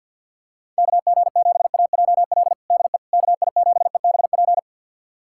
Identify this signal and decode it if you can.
Morse code, heard as a keyed tone, about 53 wpm, 700 Hz. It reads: KG7AJL DE KI7EBP
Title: USA Callsign DE Callsign - 6 Characters - 50wpm